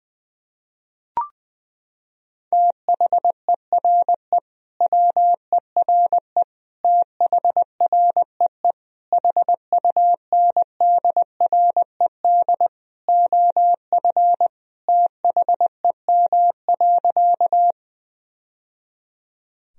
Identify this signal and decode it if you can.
Morse code, heard as a keyed tone, about 20 wpm, 700 Hz. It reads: THERE WERE THREE HUNDRED OF THEM.
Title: Sentences from Top 1000 Words - Review Part 2 - 20wpm